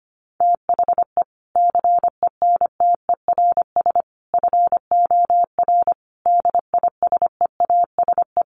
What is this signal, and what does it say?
Morse code, heard as a keyed tone, about 25 wpm, 700 Hz. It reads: THE CENTERS FOR DISEASE
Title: Sets of 5 Words Encore2 25wpm